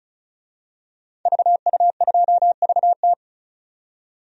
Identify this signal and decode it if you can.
Morse code, heard as a keyed tone, about 36 wpm, 700 Hz. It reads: VU2VT
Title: DXCC-100 Call Signs - 5 Characters - 35wpm